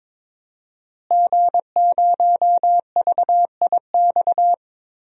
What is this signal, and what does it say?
Morse code, heard as a keyed tone, about 22 wpm, 700 Hz. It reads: G0VIX
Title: DXCC-100 Call Signs - 5 Characters - Repeated 3x - 22wpm